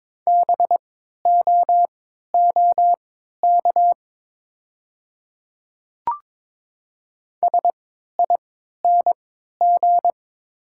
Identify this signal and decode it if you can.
Morse code, heard as a keyed tone, about 22 wpm, spaced as for 12 wpm, 700 Hz. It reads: BOOK SING
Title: 3x Character Spacing - Top 500 Words - 22wpm